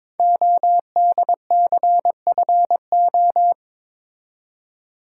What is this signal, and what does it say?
Morse code, heard as a keyed tone, about 22 wpm, 700 Hz. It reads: ODCFO
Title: Code Groups - 5 Letters - 22wpm